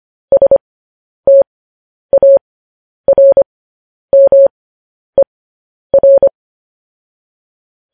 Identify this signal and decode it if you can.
Morse code, heard as a keyed tone, about 25 wpm, spaced as for 10 wpm, 550 Hz. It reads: STARMER
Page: news-headlines-in-morse-code-at-10-wpm